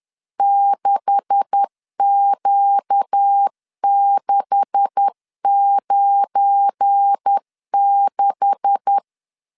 Morse code, heard as a keyed tone, about 10 wpm, 800 Hz. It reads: 6Q696
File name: LEZIONI / 9 286 10wpm 2735